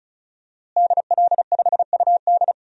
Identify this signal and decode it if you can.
Morse code, heard as a keyed tone, about 34 wpm, 700 Hz. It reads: DL5UD
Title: DXCC-100 Call Signs - 5 Characters - Repeated 2x - 35wpm